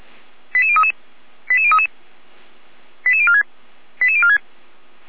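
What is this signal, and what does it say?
Five-tone selective call (ZVEI), heard as quick runs of five tones, tones of about 70 ms. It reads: 88033 88033 88046 88046
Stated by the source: Five-tone ZVEI selective call received by SDR.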